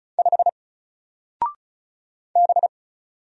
Morse code, heard as a keyed tone, about 36 wpm, 700 Hz. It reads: 5 B
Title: Single Letter-Number Alt 35wpm